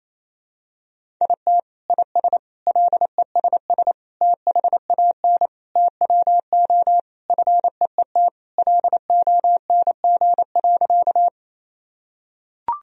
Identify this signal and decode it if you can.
Morse code, heard as a keyed tone, about 28 wpm, 700 Hz. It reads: IT IS LESS THAN TWO FEET LONG.
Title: Sentences from Top 1000 Words - Review Part 2 - 28wpm